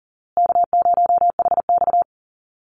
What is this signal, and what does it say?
Morse code, heard as a keyed tone, about 40 wpm, 700 Hz. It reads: K0HX
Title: Easy Call Signs Repeated 3x - 40wpm